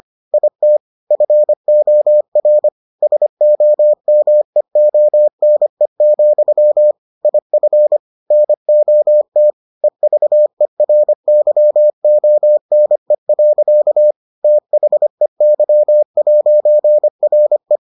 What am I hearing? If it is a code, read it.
Morse code, heard as a keyed tone, about 25 wpm, 600 Hz. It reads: IT FOR SOMEONE, IF NOT EVERYONE. THEY'RE